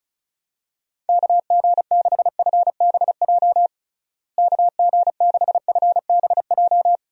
Morse code, heard as a keyed tone, about 36 wpm, 700 Hz. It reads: KG6FBJ KG6FBJ
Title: USA Call Signs - 6 Characters - Repeated 2x - 35wpm